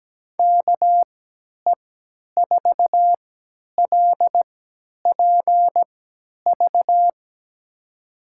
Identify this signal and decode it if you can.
Morse code, heard as a keyed tone, about 17 wpm, spaced as for 10 wpm, 700 Hz. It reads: KE4LPV
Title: USA Call Signs - 6 Characters - 3x Character Spacing - 17wpm